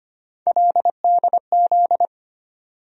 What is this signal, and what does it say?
Morse code, heard as a keyed tone, about 25 wpm, 700 Hz. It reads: LDZ
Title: USA Call Sign Postfixes - 3 Letters 25wpm